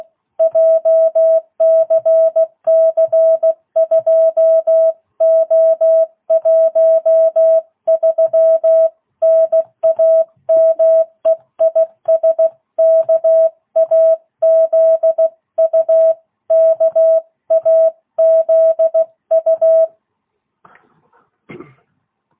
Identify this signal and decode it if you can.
Morse code, heard as a keyed tone, about 16 wpm, 660 Hz. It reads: JCC2O13 NAME IS KAZU KAZU
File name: jh8cbh / 2022cw